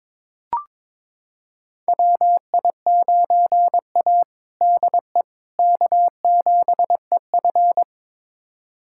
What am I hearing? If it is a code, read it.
Morse code, heard as a keyed tone, about 22 wpm, 700 Hz. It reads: WI9A DE K7EF
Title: USA Callsign DE Callsign 22wpm